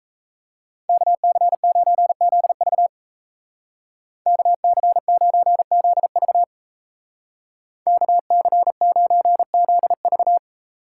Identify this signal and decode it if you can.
Morse code, heard as a keyed tone, about 33 wpm, 700 Hz. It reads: KC9ZV KC9ZV KC9ZV
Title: Speed Racing - USA 2x2 Call Signs 28wpm